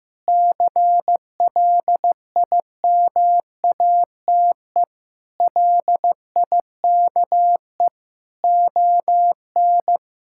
Morse code, heard as a keyed tone, about 15 wpm, 700 Hz. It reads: CLIMATE LIKE ON
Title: Sentences from Top 1600 Words 15wpm